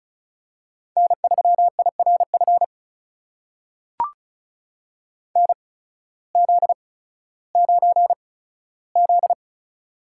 Morse code, heard as a keyed tone, about 33 wpm, spaced as for 16 wpm, 700 Hz. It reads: N3IRF NZ9Z
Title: USA Call Signs - 8x Character Spacing - 35wpm